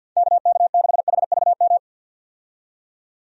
Morse code, heard as a keyed tone, about 50 wpm, 700 Hz. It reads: KK6HVM
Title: USA Call Signs - 6 Characters - 50wpm